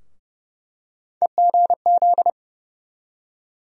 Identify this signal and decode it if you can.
Morse code, heard as a keyed tone, about 30 wpm, 700 Hz. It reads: EGZ
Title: USA Call Sign Postfixes - 3 Letters 30wpm